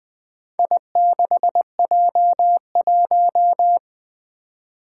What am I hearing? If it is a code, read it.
Morse code, heard as a keyed tone, about 20 wpm, 700 Hz. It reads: I6J1
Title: Four Challenging Letter-Number Combinations 20wpm